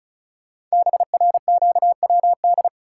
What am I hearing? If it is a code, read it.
Morse code, heard as a keyed tone, about 36 wpm, 700 Hz. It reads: BRQWD
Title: Code Groups - 5 Letters - 35wpm